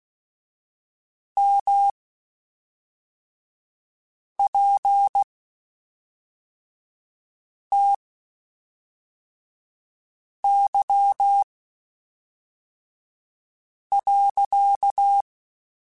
Morse code, heard as a keyed tone, about 16 wpm, 800 Hz. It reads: M P T Y .
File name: Spaced / Chuckles Spaced-mp3s / HalfFull sp @5